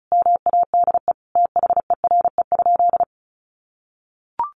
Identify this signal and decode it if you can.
Morse code, heard as a keyed tone, about 34 wpm, 700 Hz. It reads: MADE THERE?
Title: Sentences from Top 1400 Words 35wpm